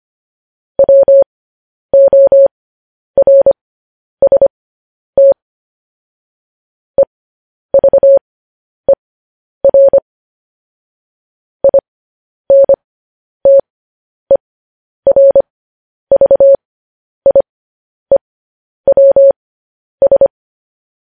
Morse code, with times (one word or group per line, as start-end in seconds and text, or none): 0.8-5.3 WORST
7.0-10.0 EVER
11.6-20.3 INTERVIEWS